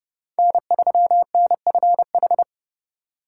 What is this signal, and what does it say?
Morse code, heard as a keyed tone, about 32 wpm, 700 Hz. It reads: N3NFH
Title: USA Call Signs - 5 Characters - 2x Character Spacing - 30wpm